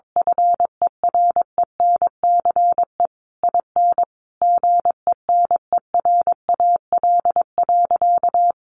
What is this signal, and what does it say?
Morse code, heard as a keyed tone, about 22 wpm, 700 Hz. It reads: FERENCE IN GENERAL.
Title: Sentences from Top 1500 Words 22wpm